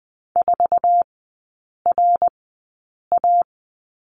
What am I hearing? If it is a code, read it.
Morse code, heard as a keyed tone, about 20 wpm, spaced as for 8 wpm, 700 Hz. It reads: 4RA
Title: 053 Lesson 10e - Letter r - Three Characters 20wpm